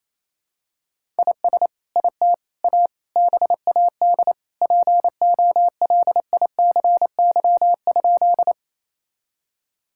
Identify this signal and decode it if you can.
Morse code, heard as a keyed tone, about 28 wpm, 700 Hz. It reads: IS IT A BAD POLICY?
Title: Sentences from Top 1000 Words - Review Part 5 - 28wpm